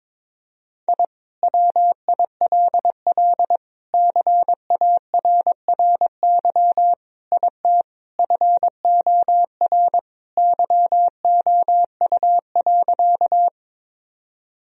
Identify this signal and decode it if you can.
Morse code, heard as a keyed tone, about 22 wpm, 700 Hz. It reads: I WILL CARRY IT FOR YOU.
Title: Sentences from Top 1000 Words - Review Part 1 - 22wpm